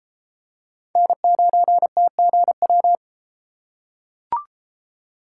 Morse code, heard as a keyed tone, about 33 wpm, 700 Hz. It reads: N9TGW